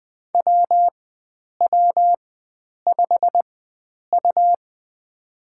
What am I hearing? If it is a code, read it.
Morse code, heard as a keyed tone, about 20 wpm, spaced as for 9 wpm, 700 Hz. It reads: WW5U
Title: USA Call Signs - 4 Characters - 4x Character Spacing - 20wpm